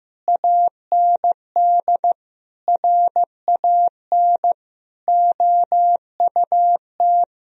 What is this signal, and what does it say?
Morse code, heard as a keyed tone, about 16 wpm, 700 Hz. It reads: AND RAN OUT
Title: Sentences from Top 1000 Words - Review Part 3 - 15wpm